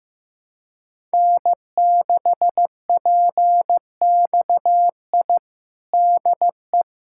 Morse code, heard as a keyed tone, about 15 wpm, 700 Hz. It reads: N6PXI DE
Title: USA Callsign DE Callsign 15wpm